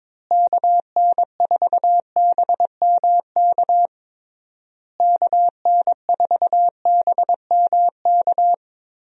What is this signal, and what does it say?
Morse code, heard as a keyed tone, about 22 wpm, 700 Hz. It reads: KN4BMK KN4BMK
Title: USA Call Signs - 6 Characters - Repeated 3x - 22wpm